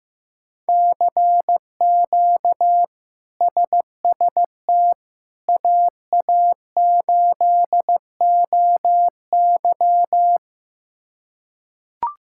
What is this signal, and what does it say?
Morse code, heard as a keyed tone, about 15 wpm, 700 Hz. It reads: CQ SST AA8OY
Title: K1USN SST QSO Practice - 15wpm